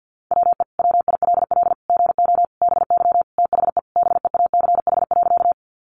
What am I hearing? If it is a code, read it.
Morse code, heard as a keyed tone, about 48 wpm, 700 Hz. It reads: WE WILL GO BY THE BEACH.
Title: Sentences from Top 1500 Words 50wpm